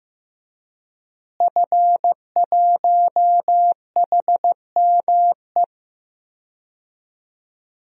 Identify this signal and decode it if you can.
Morse code, heard as a keyed tone, about 15 wpm, 700 Hz. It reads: F1HME
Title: DXCC-100 Call Signs - 5 Characters - Repeated 3x - 15wpm